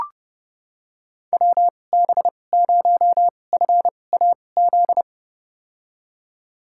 Morse code, heard as a keyed tone, about 32 wpm, spaced as for 22 wpm, 700 Hz. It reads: WB0FAZ
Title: USA Call Signs - 2x Character Spacing - 30wpm